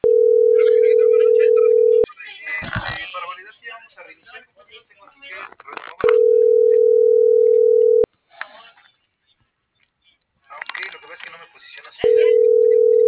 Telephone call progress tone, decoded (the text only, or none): ringback tone